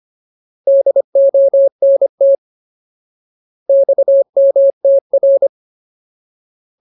Morse code, heard as a keyed tone, about 25 wpm, 550 Hz. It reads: DONT XMTR
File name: Ultimate Word List / 25WPM Ultimate